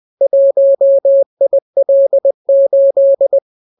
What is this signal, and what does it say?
Morse code, heard as a keyed tone, about 20 wpm, 550 Hz. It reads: 1IL8